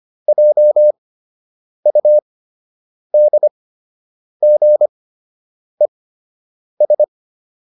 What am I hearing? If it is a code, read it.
Morse code, heard as a keyed tone, about 25 wpm, spaced as for 8 wpm, 600 Hz. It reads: JUDGES
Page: fundamental-practice-files